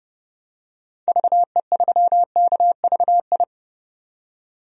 Morse code, heard as a keyed tone, about 30 wpm, 700 Hz. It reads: VE3KVI
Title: DXCC-100 Call Signs - 6 Characters - 30wpm